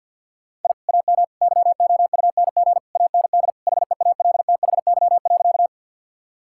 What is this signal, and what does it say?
Morse code, heard as a keyed tone, about 50 wpm, 700 Hz. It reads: I AM YOUNG AND HEALTHY.
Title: Sentences from Top 1000 Words - Review Part 5 - 50wpm